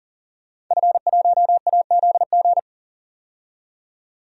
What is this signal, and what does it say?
Morse code, heard as a keyed tone, about 40 wpm, 700 Hz. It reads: F1AZG
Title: DXCC-100 Call Signs - 5 Characters - 40wpm